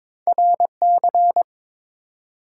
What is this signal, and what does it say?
Morse code, heard as a keyed tone, about 22 wpm, 700 Hz. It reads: RC
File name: practice / 22wpm / Two Letter-Number Combinations - 22wpm